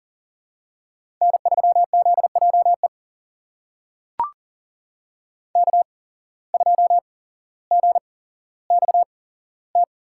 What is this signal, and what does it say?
Morse code, heard as a keyed tone, about 38 wpm, spaced as for 17 wpm, 700 Hz. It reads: N3ZJE K2GXT
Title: USA Call Signs - 8x Character Spacing - 40wpm